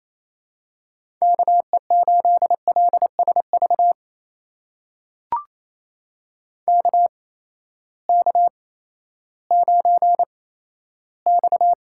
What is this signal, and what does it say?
Morse code, heard as a keyed tone, about 28 wpm, spaced as for 14 wpm, 700 Hz. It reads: KE8LSV KK9X
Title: USA Call Signs - 8x Character Spacing - 28wpm